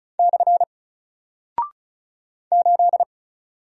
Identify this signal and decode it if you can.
Morse code, heard as a keyed tone, about 36 wpm, 700 Hz. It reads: / 8
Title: Single Character (Letter:Number:Punctuation:Prosign) Alt 35wpm